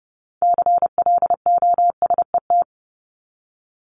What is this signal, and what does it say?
Morse code, heard as a keyed tone, about 30 wpm, 700 Hz. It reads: CLOSET
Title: Top 1751-2000 Words 30wpm